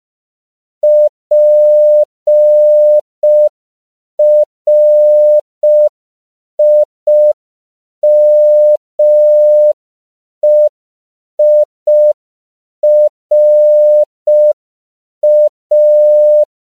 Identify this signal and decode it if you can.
Morse code, heard as a keyed tone, about 5 wpm, 600 Hz. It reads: PRIMEIRA